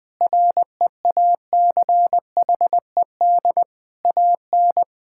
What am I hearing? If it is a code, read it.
Morse code, heard as a keyed tone, about 20 wpm, 700 Hz. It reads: REACHED AN
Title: Sentences with 5-Word Phrases - Spoken Beginning - 20wpm